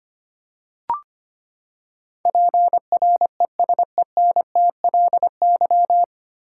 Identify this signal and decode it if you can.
Morse code, heard as a keyed tone, about 25 wpm, 700 Hz. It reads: PRESENTLY
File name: Nine Letter Words - 25wpm